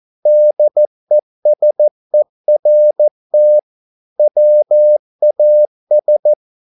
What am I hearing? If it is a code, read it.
Morse code, heard as a keyed tone, about 14 wpm, 600 Hz. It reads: DESERT WAS